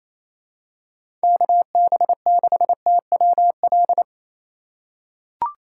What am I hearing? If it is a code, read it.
Morse code, heard as a keyed tone, about 27 wpm, 700 Hz. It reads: KB6TWL
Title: USA Call Signs - 2x Character Spacing - 28wpm